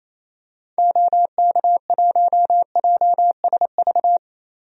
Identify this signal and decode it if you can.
Morse code, heard as a keyed tone, about 28 wpm, 700 Hz. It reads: OK1JSV
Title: DXCC-100 Call Signs - 6 Characters - Repeated 2x - 28wpm